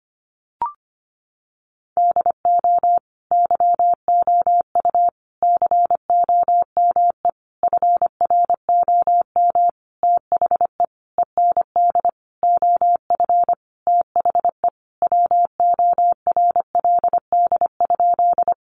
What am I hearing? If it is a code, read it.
Morse code, heard as a keyed tone, about 25 wpm, 700 Hz. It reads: DO YOU COME FROM THE END OF THE WORLD?
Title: Sentences from Top 200 Words 25wpm